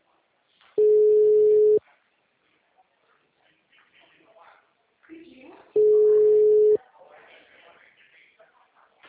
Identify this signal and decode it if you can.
Ringback tone (Europe, 425 Hz), with speech.